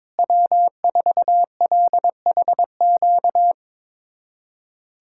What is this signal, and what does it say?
Morse code, heard as a keyed tone, about 22 wpm, 700 Hz. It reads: W4LHQ
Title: USA Callsign DE Callsign - 5 Characters - 22wpm